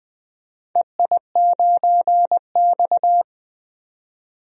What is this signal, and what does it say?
Morse code, heard as a keyed tone, about 20 wpm, 700 Hz. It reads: EI9X
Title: DXCC-100 Call Signs - 4 Characters - 20wpm